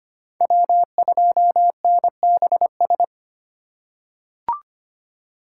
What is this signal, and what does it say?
Morse code, heard as a keyed tone, about 25 wpm, 700 Hz. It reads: W2NBS
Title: USA Call Signs - 5 Characters - 3x Character Spacing - 25wpm